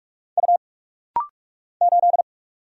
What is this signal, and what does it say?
Morse code, heard as a keyed tone, about 46 wpm, 700 Hz. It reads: U 8
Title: Single Letter-Number Alt - Rapid-Fire 45wpm